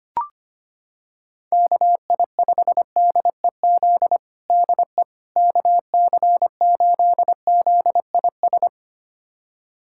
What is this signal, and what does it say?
Morse code, heard as a keyed tone, about 25 wpm, 700 Hz. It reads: KI5DEZ DE KC8ZIS
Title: USA Callsign DE Callsign 25wpm